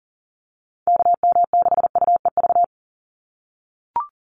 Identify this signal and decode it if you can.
Morse code, heard as a keyed tone, about 40 wpm, 700 Hz. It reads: KM6UEV
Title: USA Call Signs 40wpm